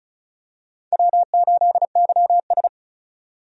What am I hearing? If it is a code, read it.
Morse code, heard as a keyed tone, about 34 wpm, 700 Hz. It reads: W8YS
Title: USA Call Signs - 4 Characters - 35wpm